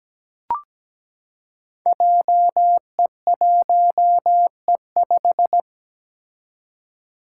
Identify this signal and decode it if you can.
Morse code, heard as a keyed tone, about 17 wpm, 700 Hz. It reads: JE1E5
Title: Five Challenging Letter-Number Combinations 17wpm